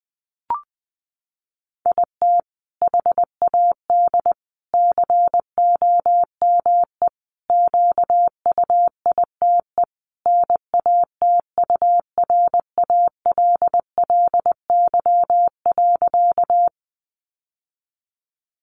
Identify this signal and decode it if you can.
Morse code, heard as a keyed tone, about 20 wpm, 700 Hz. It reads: IT HAD COME QUITE NATURALLY.
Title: Sentences from Top 1300 Words 20wpm